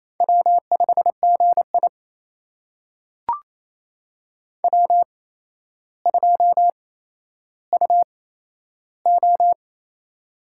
Morse code, heard as a keyed tone, about 27 wpm, spaced as for 12 wpm, 700 Hz. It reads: W5GI W2UO
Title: USA Call Signs - 4 Characters - 8x Character Spacing - 28wpm